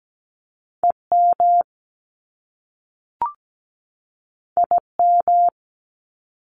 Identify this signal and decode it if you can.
Morse code, heard as a keyed tone, about 17 wpm, 700 Hz. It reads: EM IM
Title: Top Two Letter Combinations in Words 17wpm